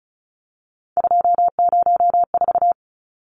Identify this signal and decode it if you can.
Morse code, heard as a keyed tone, about 34 wpm, 700 Hz. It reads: 204
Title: Numbers - 3-Digit 35wpm